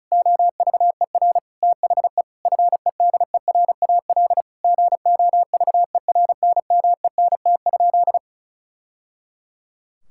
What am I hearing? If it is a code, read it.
Morse code, heard as a keyed tone, about 36 wpm, 700 Hz. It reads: OVER THE FEDERAL GOVERNMENT?
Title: Sentences from Top 1000 Words - Review Part 5 - 35wpm